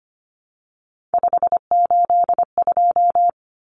Morse code, heard as a keyed tone, about 25 wpm, 700 Hz. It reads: 582
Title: Numbers - 3-Digit 25wpm